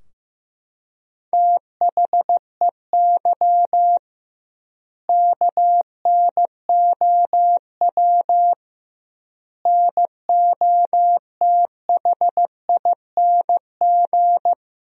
Morse code, heard as a keyed tone, about 15 wpm, 700 Hz. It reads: THEY KNOW NOTHING
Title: 2x Word-Spacing - Sentences from Top 500 Words - 15wpm